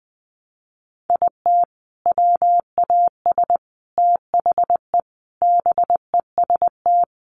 Morse code, heard as a keyed tone, about 20 wpm, 700 Hz. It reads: IT WAS THE BEST
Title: Sentences from Top 400 Words 20wpm